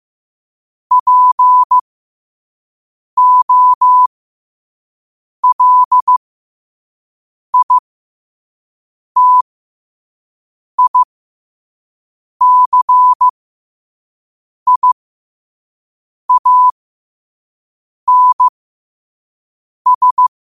Morse code, heard as a keyed tone, about 15 wpm, spaced as for 5 wpm, 1000 Hz. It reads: POLITICIANS
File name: qotdfiles / 5wpm